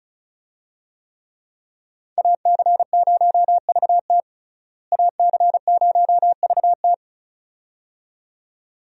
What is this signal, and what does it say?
Morse code, heard as a keyed tone, about 34 wpm, 700 Hz. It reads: AC0VT AC0VT
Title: USA Call Signs - 5 Characters - Repeated 2x - 35wpm